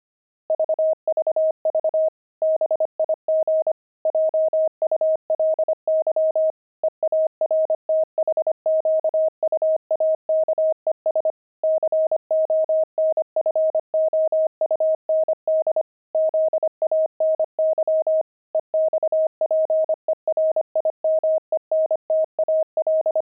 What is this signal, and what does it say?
Morse code, heard as a keyed tone, about 25 wpm, 630 Hz. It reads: VVV BIG JULY EARTHQUAKES CONFOUND ZANY EXPERIMENTAL
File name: pangram-fs-25-cs-25-zany